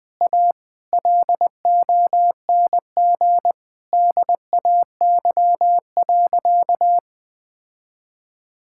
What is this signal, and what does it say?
Morse code, heard as a keyed tone, about 21 wpm, 700 Hz. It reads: A LONG DAY.
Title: Sentences with 4-Word Phrases - Spoken Beginning - 20wpm